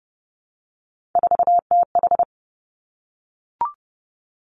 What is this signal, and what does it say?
Morse code, heard as a keyed tone, about 32 wpm, 700 Hz. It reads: 4TH
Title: Three Challenging Letter-Number Combinations 30wpm